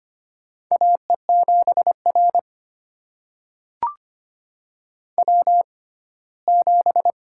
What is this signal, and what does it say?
Morse code, heard as a keyed tone, about 25 wpm, spaced as for 17 wpm, 700 Hz. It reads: AE7R W7
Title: USA Call Signs - 6x Character Spacing - 25wpm